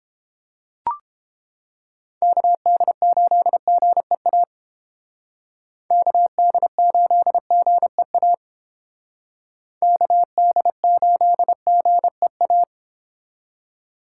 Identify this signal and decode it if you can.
Morse code, heard as a keyed tone, about 29 wpm, 700 Hz. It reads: KD8GEA KD8GEA KD8GEA